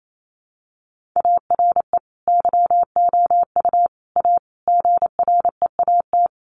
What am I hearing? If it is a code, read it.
Morse code, heard as a keyed tone, about 28 wpm, 700 Hz. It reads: ARE YOU A GREAT
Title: Sentences from Top 1500 Words 28wpm